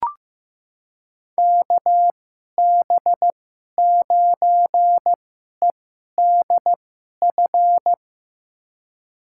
Morse code, heard as a keyed tone, about 16 wpm, spaced as for 11 wpm, 700 Hz. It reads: KB9EDF
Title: USA Call Signs - 6 Characters - 2x Character Spacing - 15wpm